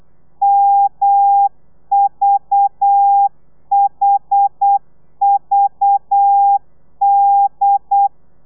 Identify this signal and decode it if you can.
Morse code, heard as a keyed tone, about 8 wpm, 800 Hz. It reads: MVHVD
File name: telegrafia / i2tfj / Lez11 35